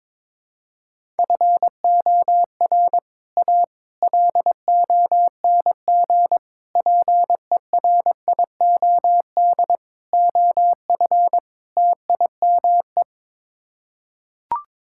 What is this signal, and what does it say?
Morse code, heard as a keyed tone, about 22 wpm, 700 Hz. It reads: FOR A LONG PERIOD OF TIME
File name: practice / 22wpm / Sets of 6 Words - 22wpm